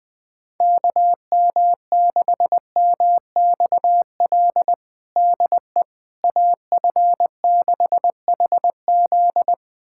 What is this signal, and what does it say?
Morse code, heard as a keyed tone, about 20 wpm, 700 Hz. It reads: KM6MXL DE AF6HZ
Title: USA Callsign DE Callsign 20wpm